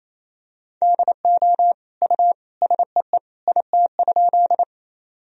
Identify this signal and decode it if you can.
Morse code, heard as a keyed tone, about 28 wpm, 700 Hz. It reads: DO U SEE IT?
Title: Sentences from Top 100 Words and Abbreviations 28wpm